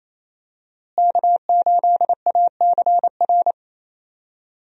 Morse code, heard as a keyed tone, about 28 wpm, 700 Hz. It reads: K8ACR
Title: USA Call Signs - 5 Characters - Repeated 2x - 28wpm